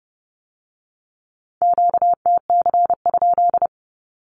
Morse code, heard as a keyed tone, about 30 wpm, 700 Hz. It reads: QTC?
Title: Q-Codes with Spoken Meaning - 30wpm